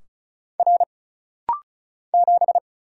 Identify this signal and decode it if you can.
Morse code, heard as a keyed tone, about 34 wpm, 700 Hz. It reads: R 7
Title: Single Letter-Number Alt - Rapid-Fire 35wpm